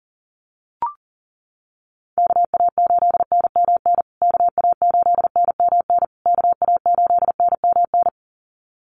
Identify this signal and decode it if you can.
Morse code, heard as a keyed tone, about 40 wpm, 700 Hz. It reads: KA8NMN KA8NMN KA8NMN
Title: USA Call Signs Repeated 3x 40wpm